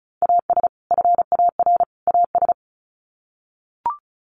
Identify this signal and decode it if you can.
Morse code, heard as a keyed tone, about 34 wpm, 700 Hz. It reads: AS FAR AS
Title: Sets of 3 Words 35wpm